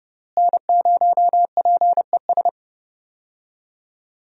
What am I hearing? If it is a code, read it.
Morse code, heard as a keyed tone, about 30 wpm, 700 Hz. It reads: N0PES
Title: USA Call Signs - 5 Characters - Repeated 3x - 30wpm